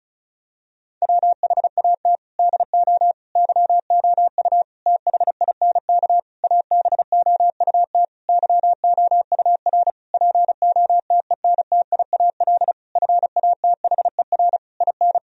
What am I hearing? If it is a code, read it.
Morse code, heard as a keyed tone, about 34 wpm, 700 Hz. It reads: WHAT DO YOU THINK ABOUT YOUR POTENTIAL FATHER IN
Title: Sentences from Top 1000 Words - Review Part 4 - 35wpm